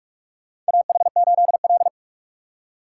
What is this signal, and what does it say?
Morse code, heard as a keyed tone, about 46 wpm, 700 Hz. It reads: AH8L
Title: USA Call Signs - 4 Characters - 3x Character Spacing - 45wpm